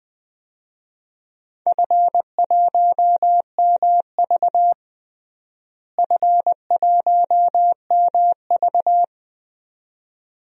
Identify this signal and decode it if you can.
Morse code, heard as a keyed tone, about 21 wpm, 700 Hz. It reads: F1MV F1MV
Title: DXCC-100 Call Signs - 4 Characters - Repeated 2x - 20wpm